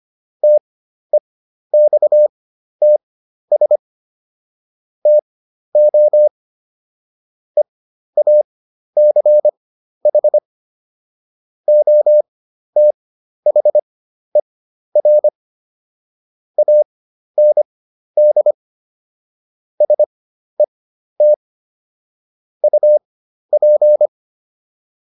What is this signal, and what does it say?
Morse code, heard as a keyed tone, about 25 wpm, spaced as for 12 wpm, 600 Hz. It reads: TEXTS TO EACH OTHER AND SET UP